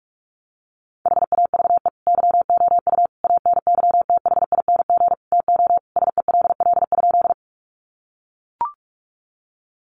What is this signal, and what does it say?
Morse code, heard as a keyed tone, about 44 wpm, 700 Hz. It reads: HAVE YOU ANYTHING TO SELL?
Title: Sentences from Top 1100 Words 45wpm